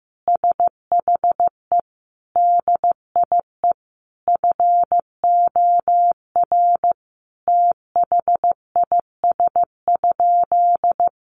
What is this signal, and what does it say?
Morse code, heard as a keyed tone, about 15 wpm, 700 Hz. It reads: SHE DIE FOR THIS?
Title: Sentences from Top 1300 Words 15wpm